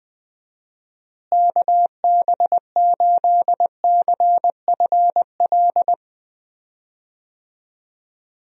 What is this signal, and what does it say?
Morse code, heard as a keyed tone, about 20 wpm, 700 Hz. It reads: KB8CFL
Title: USA Call Signs - 6 Characters - Repeated 2x - 20wpm